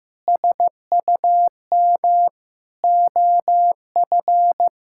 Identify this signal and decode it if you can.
Morse code, heard as a keyed tone, about 16 wpm, 700 Hz. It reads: SUM OF
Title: Sentences from Top 1000 Words - Review Part 6 - 15wpm